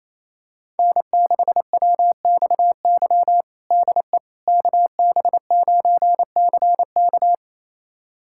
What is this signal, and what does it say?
Morse code, heard as a keyed tone, about 28 wpm, 700 Hz. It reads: N6WXY DE KB9CK
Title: USA Callsign DE Callsign - 5 Characters - 28wpm